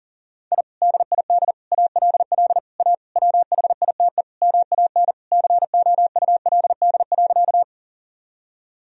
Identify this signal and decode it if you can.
Morse code, heard as a keyed tone, about 37 wpm, 700 Hz. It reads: I DID ALL A WHITE MAN COULD.
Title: Sentences from Top 1000 Words - Review Part 1 - 40wpm